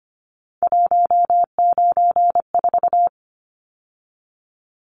Morse code, heard as a keyed tone, about 25 wpm, 700 Hz. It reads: 194
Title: Numbers - 3-Digit 25wpm